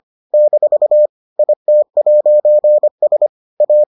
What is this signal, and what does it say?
Morse code, heard as a keyed tone, about 25 wpm, 600 Hz. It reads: - IT'S A